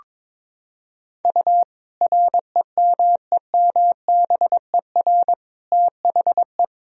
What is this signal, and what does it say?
Morse code, heard as a keyed tone, about 22 wpm, 700 Hz. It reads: U REMEMBER THE
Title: Sentences from Top 400 Words and Abbreviations 22wpm